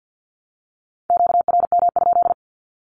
Morse code, heard as a keyed tone, about 50 wpm, 700 Hz. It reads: QRM?
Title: Q-Codes Repeated 3x 50wpm